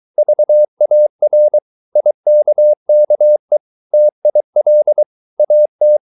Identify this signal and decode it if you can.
Morse code, heard as a keyed tone, about 23 wpm, 600 Hz. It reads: VAR IKKE TIL AT
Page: Ferie 23 wpm